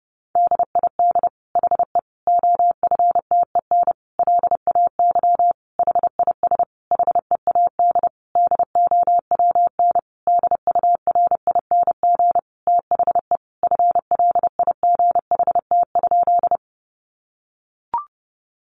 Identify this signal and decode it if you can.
Morse code, heard as a keyed tone, about 30 wpm, 700 Hz. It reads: DID HE OFTEN LAY HIS HEAD DOWN DURING THE FLIGHT?
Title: Sentences from Top 1000 Words 30wpm